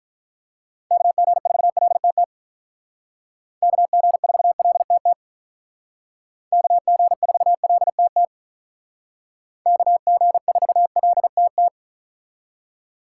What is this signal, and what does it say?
Morse code, heard as a keyed tone, about 45 wpm, 700 Hz. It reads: KG4LTT KG4LTT KG4LTT KG4LTT